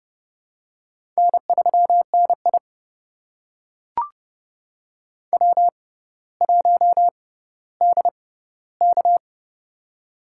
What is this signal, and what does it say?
Morse code, heard as a keyed tone, about 30 wpm, spaced as for 15 wpm, 700 Hz. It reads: N3NI W1DK
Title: USA Call Signs - 6x Character Spacing - 30wpm